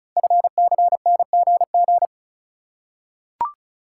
Morse code, heard as a keyed tone, about 36 wpm, 700 Hz. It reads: FCNGG
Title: Code Groups - 5 Letters - 35wpm